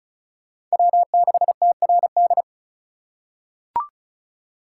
Morse code, heard as a keyed tone, about 36 wpm, 700 Hz. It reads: W6TRD
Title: USA Call Signs - 3x Character Spacing - 35wpm